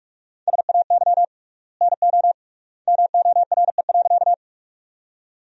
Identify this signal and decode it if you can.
Morse code, heard as a keyed tone, about 46 wpm, 700 Hz. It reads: SAY NO MORE.
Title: 3x Word-Spacing - Sentences from Top 200 Words - 45wpm